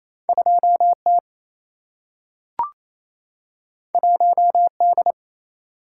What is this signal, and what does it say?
Morse code, heard as a keyed tone, about 28 wpm, 700 Hz. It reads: 2 T 1 D
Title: Single-Letter Single-Number 28wpm